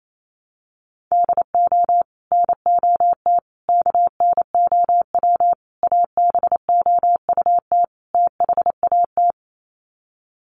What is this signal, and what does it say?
Morse code, heard as a keyed tone, about 28 wpm, 700 Hz. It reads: DO NOT KNOW ABOUT THAT
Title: Sets of 5 Words Encore2 28wpm